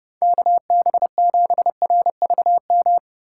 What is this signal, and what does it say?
Morse code, heard as a keyed tone, about 32 wpm, 700 Hz. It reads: KB7RVM